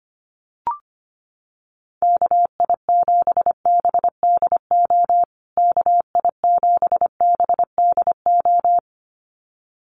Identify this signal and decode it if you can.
Morse code, heard as a keyed tone, about 25 wpm, 700 Hz. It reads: KI7BDO KI7BDO
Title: USA Call Signs Repeated 2x 25wpm